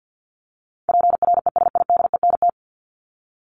Morse code, heard as a keyed tone, about 50 wpm, 700 Hz. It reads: PRESIDENT
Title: Top 751-1000 Words 50wpm